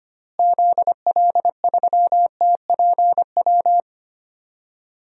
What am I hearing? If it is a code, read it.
Morse code, heard as a keyed tone, about 25 wpm, 700 Hz. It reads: ZL3TPW
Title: DXCC-100 Call Signs - 6 Characters - Repeated 3x - 25wpm